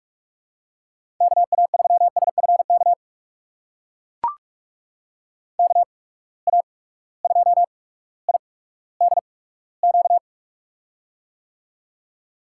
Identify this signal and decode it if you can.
Morse code, heard as a keyed tone, about 43 wpm, spaced as for 19 wpm, 700 Hz. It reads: KA3SFK KA2IDQ
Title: USA Call Signs - 6 Characters - 8x Character Spacing - 45wpm